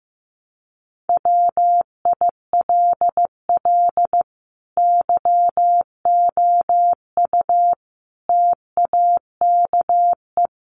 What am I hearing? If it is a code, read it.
Morse code, heard as a keyed tone, about 15 wpm, 700 Hz. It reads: WILL YOU TAKE
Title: Sentences from Top 900 Words 15wpm